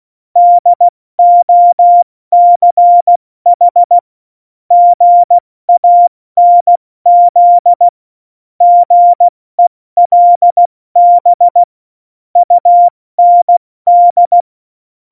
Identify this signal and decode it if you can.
Morse code, heard as a keyed tone, about 16 wpm, 700 Hz. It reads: DOCH GANZ GELB UND
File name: Morsekurs / Witze / Witz 21. August 2018, 70 F80